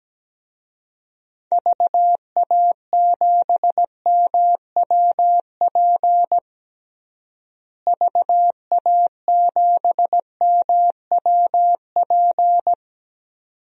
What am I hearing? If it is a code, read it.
Morse code, heard as a keyed tone, about 17 wpm, 700 Hz. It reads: VA7MWP VA7MWP
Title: DXCC-100 Call Signs - 6 Characters - Repeated 2x - 17wpm